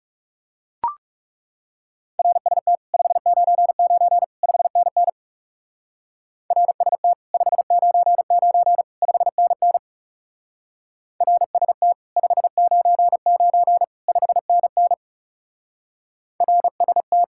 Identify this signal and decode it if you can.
Morse code, heard as a keyed tone, about 36 wpm, 700 Hz. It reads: RST 599 5NN RST 599 5NN RST 599 5NN RST
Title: Speed-Racing Common QSO Phrases - 30wpm